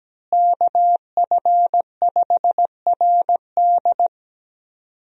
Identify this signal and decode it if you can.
Morse code, heard as a keyed tone, about 17 wpm, 700 Hz. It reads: KF5RD
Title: USA Call Signs - 5 Characters - Repeated 2x - 17wpm